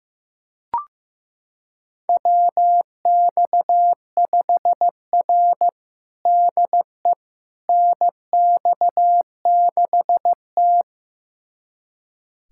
Morse code, heard as a keyed tone, about 15 wpm, 700 Hz. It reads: WX5R DE NX6T
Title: USA Callsign DE Callsign - 4 Characters - 15wpm